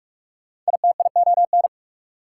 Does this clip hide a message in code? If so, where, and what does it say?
Yes: Morse code from 0.7 to 1.7 s, heard as a keyed tone, about 58 wpm, 690 Hz. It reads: ITION